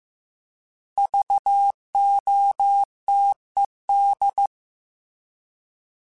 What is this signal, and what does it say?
Morse code, heard as a keyed tone, about 15 wpm, 800 Hz. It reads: VOTED
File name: Alphabetical / Alphabetical mp3s / Vs @13